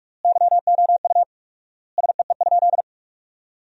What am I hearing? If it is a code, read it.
Morse code, heard as a keyed tone, about 46 wpm, 700 Hz. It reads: YOU SEE?
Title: 4x Word-Spacing - Sentences from Top 100 Words - 45wpm